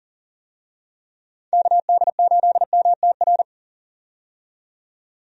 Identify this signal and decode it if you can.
Morse code, heard as a keyed tone, about 37 wpm, 700 Hz. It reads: KD8MTR
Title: USA Call Signs - 6 Characters - 40wpm